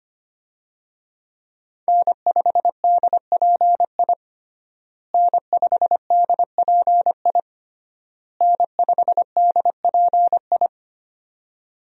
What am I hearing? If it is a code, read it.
Morse code, heard as a keyed tone, about 25 wpm, 700 Hz. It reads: N5DPI N5DPI N5DPI
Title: USA Call Signs - 5 Characters - Repeated 3x - 25wpm